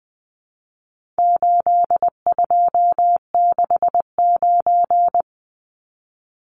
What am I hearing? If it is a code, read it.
Morse code, heard as a keyed tone, about 19 wpm, 700 Hz. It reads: 8269
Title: Numbers - 4-Digit 20wpm